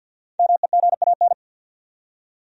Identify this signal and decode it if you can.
Morse code, heard as a keyed tone, about 51 wpm, 700 Hz. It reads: MEGAN
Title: USA Common Names - Max 5 Letters - 50wpm